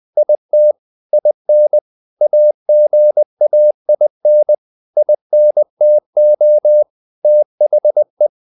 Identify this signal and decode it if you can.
Morse code, heard as a keyed tone, about 20 wpm, 600 Hz. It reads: IT IN AGAIN INTO THE
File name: the old man and the sea part 2 20wpm